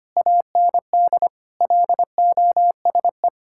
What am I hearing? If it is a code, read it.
Morse code, heard as a keyed tone, about 25 wpm, 700 Hz. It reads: AND LOSE
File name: practice / 25wpm / Difficult Sentences with Hints - 25wpm